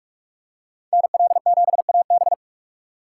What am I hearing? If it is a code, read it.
Morse code, heard as a keyed tone, about 47 wpm, 700 Hz. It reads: NL7AB
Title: USA Call Signs - 5 Characters - 4x Character Spacing - 45wpm